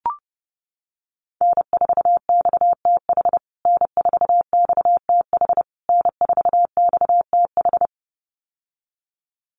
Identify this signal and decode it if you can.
Morse code, heard as a keyed tone, about 30 wpm, 700 Hz. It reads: N4XTH N4XTH N4XTH
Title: USA Call Signs Repeated 3x 30wpm